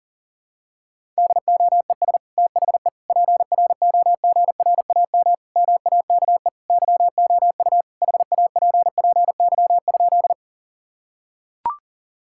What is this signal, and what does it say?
Morse code, heard as a keyed tone, about 40 wpm, 700 Hz. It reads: DOES THE PROGRAM MAKE YOU HAPPY?
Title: Sentences from Top 1000 Words - Review Part 5 - 40wpm